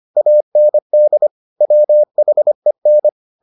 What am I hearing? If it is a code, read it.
Morse code, heard as a keyed tone, about 25 wpm, 600 Hz. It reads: AND WHEN